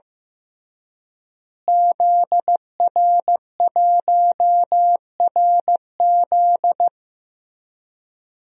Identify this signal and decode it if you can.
Morse code, heard as a keyed tone, about 15 wpm, 700 Hz. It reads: ZR1RZ
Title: DXCC-100 Call Signs - 5 Characters - Repeated 3x - 15wpm